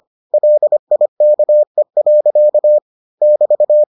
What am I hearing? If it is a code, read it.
Morse code, heard as a keyed tone, about 25 wpm, 600 Hz. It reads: LIKE. <BT>